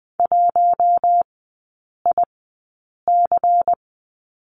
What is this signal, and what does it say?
Morse code, heard as a keyed tone, about 20 wpm, spaced as for 8 wpm, 700 Hz. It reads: 1IC
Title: 127 Lesson 21e - Number 6 - Three Characters 20wpm